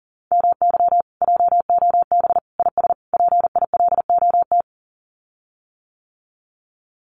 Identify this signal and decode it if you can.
Morse code, heard as a keyed tone, about 40 wpm, 700 Hz. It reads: MY JOB IS PILOT
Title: Common QSO Phrases Encore 40wpm